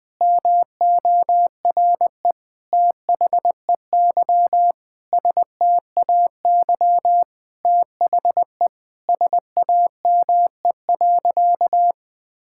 Morse code, heard as a keyed tone, about 20 wpm, 700 Hz. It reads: MORE THEY STAY THE SAME.
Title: Sentences with 9-Word Phrases - 20wpm